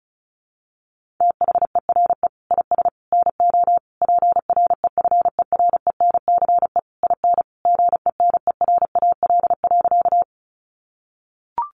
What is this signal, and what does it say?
Morse code, heard as a keyed tone, about 34 wpm, 700 Hz. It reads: THERE IS NO PREFERENCE IN GENERAL.
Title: Sentences from Top 1500 Words 35wpm